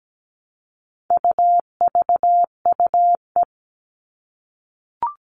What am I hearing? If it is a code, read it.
Morse code, heard as a keyed tone, about 17 wpm, 700 Hz. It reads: UVUE
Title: All Cut Numbers - 4-Digit 17wpm